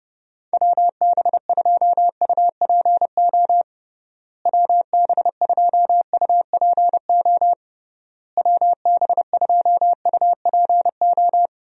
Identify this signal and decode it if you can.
Morse code, heard as a keyed tone, about 32 wpm, 700 Hz. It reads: WB2UPO WB2UPO WB2UPO
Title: USA Call Signs - 6 Characters - Repeated 3x - 30wpm